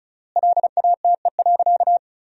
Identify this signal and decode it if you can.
Morse code, heard as a keyed tone, about 34 wpm, 700 Hz. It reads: LATE.
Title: 3x Word-Spacing - Sentences from Top 300 Words - 35wpm